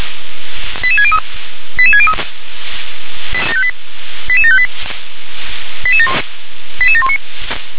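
Five-tone selective call (ZVEI), heard as quick runs of five tones, tones of about 70 ms. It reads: 88603 88603 88758 88719
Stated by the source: ZVEI five-tone call 88714 received on SDR.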